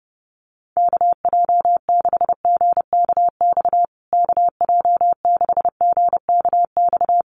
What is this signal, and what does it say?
Morse code, heard as a keyed tone, about 30 wpm, 700 Hz. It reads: KJ6GKX KJ6GKX
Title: USA Call Signs Repeated 3x 30wpm